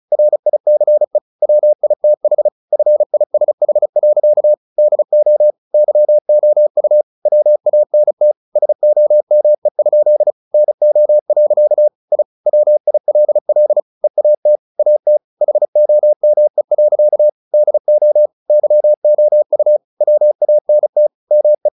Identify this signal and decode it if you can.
Morse code, heard as a keyed tone, about 34 wpm, 600 Hz. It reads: RICE WITH FISH. DO YOU WANT SOME? NO. I WILL EAT AT HOME. DO YOU WANT ME
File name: the old man and the sea / the old man and the sea 1 35wpm